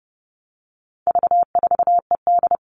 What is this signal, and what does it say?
Morse code, heard as a keyed tone, about 30 wpm, 700 Hz. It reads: V4ED
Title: All Cut Numbers - 4-Digit 30wpm